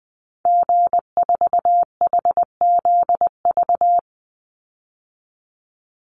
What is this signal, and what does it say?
Morse code, heard as a keyed tone, about 20 wpm, 700 Hz. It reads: G4HZV
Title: Call Signs Repeated 3x - 20wpm